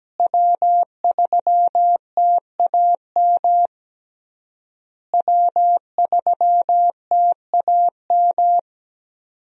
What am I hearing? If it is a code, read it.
Morse code, heard as a keyed tone, about 17 wpm, 700 Hz. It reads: W3TAM W3TAM
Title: USA Call Signs - 5 Characters - Repeated 3x - 17wpm